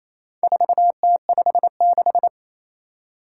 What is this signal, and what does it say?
Morse code, heard as a keyed tone, about 27 wpm, 700 Hz. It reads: 4T56
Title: Four Challenging Letter-Number Combinations 28wpm